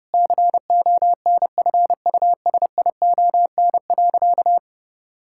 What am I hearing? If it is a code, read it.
Morse code, heard as a keyed tone, about 32 wpm, 700 Hz. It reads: CONFUSION.
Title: Sentences from Top 1600 Words 30wpm